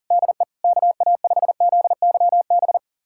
Morse code, heard as a keyed tone, about 40 wpm, 700 Hz. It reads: DE KA5ZYB
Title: USA Callsign DE Callsign - 6 Characters - 40wpm